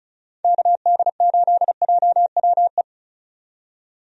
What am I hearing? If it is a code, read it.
Morse code, heard as a keyed tone, about 36 wpm, 700 Hz. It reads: KD8JWE
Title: USA Call Signs - 6 Characters - 35wpm